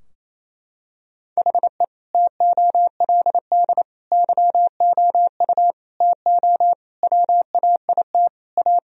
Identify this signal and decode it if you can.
Morse code, heard as a keyed tone, about 28 wpm, 700 Hz. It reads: HE TOLD YOU TO WAIT A
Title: Sentences from Top 1000 Words - Review Part 2 - 28wpm